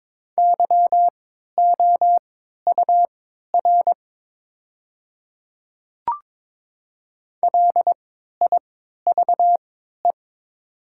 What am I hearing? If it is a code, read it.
Morse code, heard as a keyed tone, about 22 wpm, spaced as for 13 wpm, 700 Hz. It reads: YOUR LIVE
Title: 3x Character Spacing - Top 300 Words - 22wpm